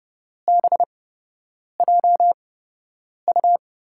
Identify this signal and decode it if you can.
Morse code, heard as a keyed tone, about 30 wpm, spaced as for 8 wpm, 700 Hz. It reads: BJU
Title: USA Call Signs - 6 Characters - 8x Character Spacing - 30wpm